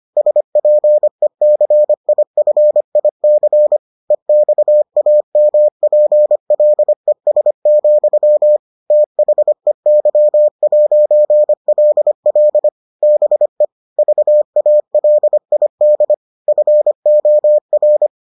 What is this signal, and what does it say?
Morse code, heard as a keyed tone, about 25 wpm, 600 Hz. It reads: SPECIFIC EXAMPLES, THEY'LL BE VALID FOR